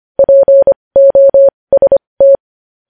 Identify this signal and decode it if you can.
Morse code, heard as a keyed tone, about 25 wpm, spaced as for 20 wpm, 550 Hz. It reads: POST
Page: news headlines in morse code at 20 wpm